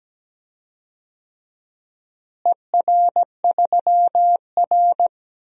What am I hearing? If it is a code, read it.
Morse code, heard as a keyed tone, about 17 wpm, 700 Hz. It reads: ER3R
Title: DXCC-100 Call Signs - 4 Characters - Repeated 3x - 17wpm